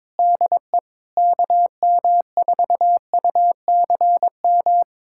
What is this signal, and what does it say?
Morse code, heard as a keyed tone, about 22 wpm, 700 Hz. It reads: DE KM4UCM
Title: USA Callsign DE Callsign - 6 Characters - 22wpm